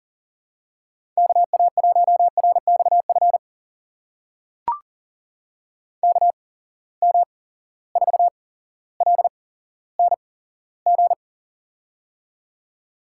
Morse code, heard as a keyed tone, about 40 wpm, spaced as for 17 wpm, 700 Hz. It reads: KA1RXF KM4LNG
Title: USA Call Signs - 6 Characters - 8x Character Spacing - 40wpm